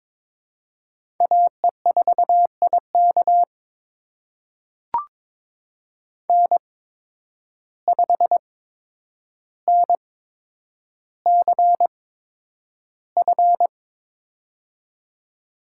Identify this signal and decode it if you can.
Morse code, heard as a keyed tone, about 22 wpm, spaced as for 9 wpm, 700 Hz. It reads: AE4IK N5NCF
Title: USA Call Signs - 5 Characters - 8x Character Spacing - 22wpm